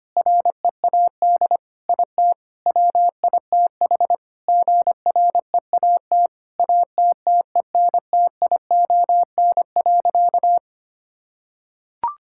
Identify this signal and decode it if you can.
Morse code, heard as a keyed tone, about 25 wpm, 700 Hz. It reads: READ IT WITH GREAT ATTENTION.
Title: Sentences from Top 1000 Words - Review Part 6 - 25wpm